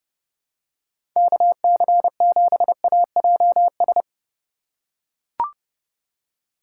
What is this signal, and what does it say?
Morse code, heard as a keyed tone, about 32 wpm, 700 Hz. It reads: KC7AJS
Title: USA Call Signs - 3x Character Spacing - 30wpm